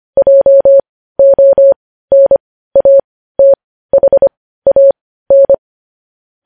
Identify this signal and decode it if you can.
Morse code, heard as a keyed tone, about 25 wpm, spaced as for 15 wpm, 550 Hz. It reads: JONATHAN
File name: podcast / News headlines-1x-25x15